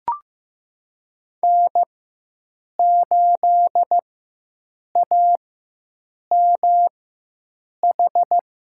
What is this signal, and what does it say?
Morse code, heard as a keyed tone, about 15 wpm, spaced as for 7 wpm, 700 Hz. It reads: N8AMH
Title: USA Call Signs - 4x Character Spacing - 15wpm